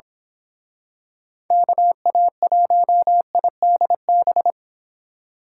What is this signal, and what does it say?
Morse code, heard as a keyed tone, about 28 wpm, 700 Hz. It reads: KA1IDB